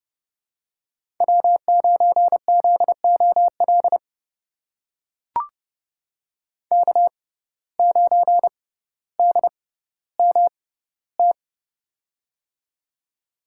Morse code, heard as a keyed tone, about 32 wpm, spaced as for 16 wpm, 700 Hz. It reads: W9ZOL K9DMT
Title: USA Call Signs - 5 Characters - 6x Character Spacing - 30wpm